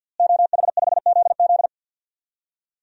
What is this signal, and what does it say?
Morse code, heard as a keyed tone, about 53 wpm, 700 Hz. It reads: OH5ZZ
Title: DXCC-100 Call Signs - 5 Characters - Repeated 3x - 50wpm